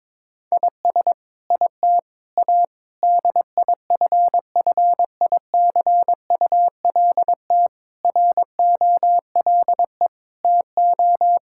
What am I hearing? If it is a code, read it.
Morse code, heard as a keyed tone, about 22 wpm, 700 Hz. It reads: IS IT A DIFFICULT ROLE TO
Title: Sentences from Top 1000 Words - Review Part 6 - 22wpm